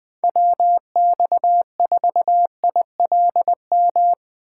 Morse code, heard as a keyed tone, about 20 wpm, 700 Hz. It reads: WX4ILM